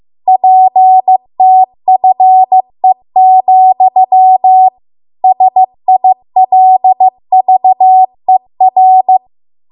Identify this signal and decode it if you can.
Morse code, heard as a keyed tone, about 15 wpm, 750 Hz. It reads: PTFE, SILVER